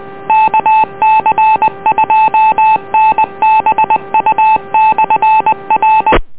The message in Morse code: KC2NBU/R